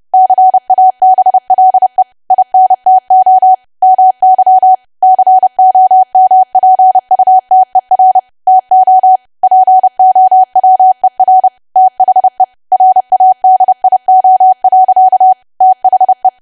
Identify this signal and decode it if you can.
Morse code, heard as a keyed tone, about 30 wpm, 750 Hz. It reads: CABLE INTO MY COMPUTER TO POWER THE RADIO. THE
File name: cwa / W1AW-Level3 / 30WPM